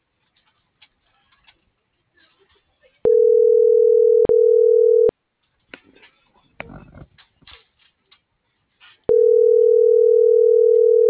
Ringback tone (North America), 440 + 480 Hz.
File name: CENTRAL SKYBLUE / out lorena.gonzalez